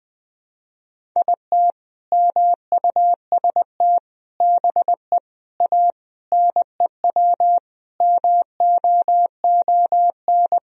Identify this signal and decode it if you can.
Morse code, heard as a keyed tone, about 20 wpm, 700 Hz. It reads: IT MUST BE A NEW MOON
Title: Sentences from Top 500 Words and Abbreviations 20wpm